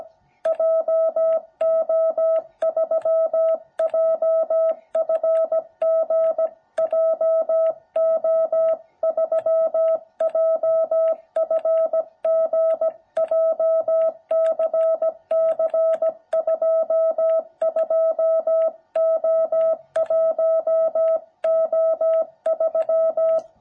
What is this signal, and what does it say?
Morse code, heard as a keyed tone, about 17 wpm, 660 Hz. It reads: JO3JFG JO3JFG JCC22O1O3
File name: jh8cbh / 2021cw